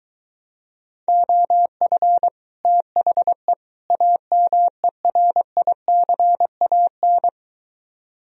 Morse code, heard as a keyed tone, about 23 wpm, 700 Hz. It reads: OF THE AMERICAN